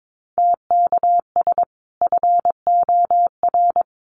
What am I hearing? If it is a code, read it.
Morse code, heard as a keyed tone, about 22 wpm, 700 Hz. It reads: TKS FOR
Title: Common QSO Phrases 22wpm